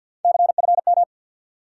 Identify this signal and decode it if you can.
Morse code, heard as a keyed tone, about 53 wpm, 700 Hz. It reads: CFM
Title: Top 90 QSO Elements 50wpm